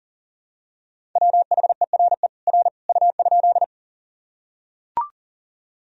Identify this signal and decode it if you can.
Morse code, heard as a keyed tone, about 40 wpm, 700 Hz. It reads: WHERE R U?
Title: Sentences from Top 200 Words and Abbreviations 40wpm